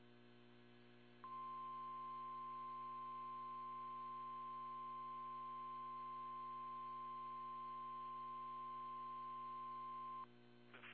Emergency Alert System (SAME) tones (the 1050 Hz attention tone), with noise. It is heard from 1.2 to 10.3 s.